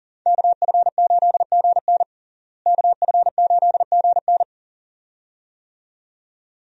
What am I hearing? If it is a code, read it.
Morse code, heard as a keyed tone, about 40 wpm, 700 Hz. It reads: KF8GN KF8GN
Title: USA Call Signs - 5 Characters - Repeated 2x - 40wpm